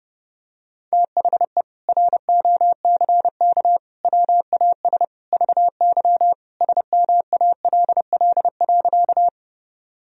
Morse code, heard as a keyed tone, about 30 wpm, 700 Hz. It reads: THE ROCK WAS VY SMALL.
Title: Sentences from Top 400 Words and Abbreviations 30wpm